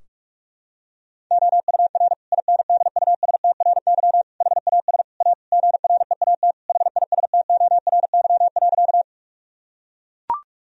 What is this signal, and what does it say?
Morse code, heard as a keyed tone, about 43 wpm, 700 Hz. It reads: OUR INDUSTRY HAS A GREAT HISTORY.
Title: Sentences from Top 1000 Words - Review Part 4 - 45wpm